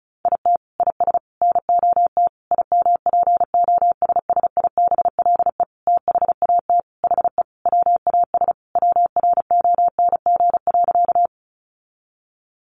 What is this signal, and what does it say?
Morse code, heard as a keyed tone, about 34 wpm, 700 Hz. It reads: IT IS NOT IMPOSSIBLE THAT HE WAS WRONG.
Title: Sentences from Top 1100 Words 35wpm